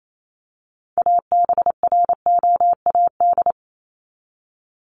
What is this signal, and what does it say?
Morse code, heard as a keyed tone, about 28 wpm, 700 Hz. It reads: ABROAD
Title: Top 1751-2000 Words 28wpm